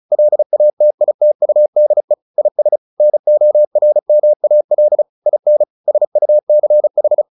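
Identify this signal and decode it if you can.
Morse code, heard as a keyed tone, about 34 wpm, 600 Hz. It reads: LATITUDE IS NORMAL IN SUCH
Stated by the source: Morse practice at 35 wpm from CW med Gnister, sending Ken Follett's novel Never in English.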